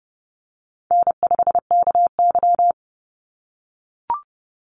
Morse code, heard as a keyed tone, about 30 wpm, 700 Hz. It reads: N5KY
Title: Call Signs - 30wpm